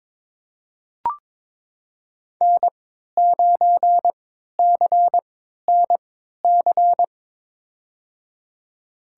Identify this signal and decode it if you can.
Morse code, heard as a keyed tone, about 22 wpm, spaced as for 12 wpm, 700 Hz. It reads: N9CNC
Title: USA Call Signs - 5 Characters - 3x Character Spacing - 22wpm